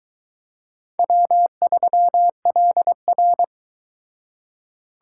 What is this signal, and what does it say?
Morse code, heard as a keyed tone, about 22 wpm, 700 Hz. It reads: W3LR